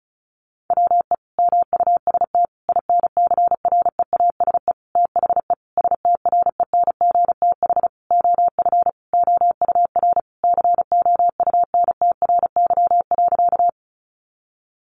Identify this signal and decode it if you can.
Morse code, heard as a keyed tone, about 34 wpm, 700 Hz. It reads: WE MUST INCREASE THE STRENGTH OF OUR COUNTRY.
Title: Sentences from Top 1300 Words 35wpm